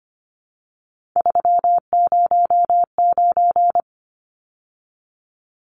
Morse code, heard as a keyed tone, about 25 wpm, 700 Hz. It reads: 309
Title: Numbers - 3-Digit 25wpm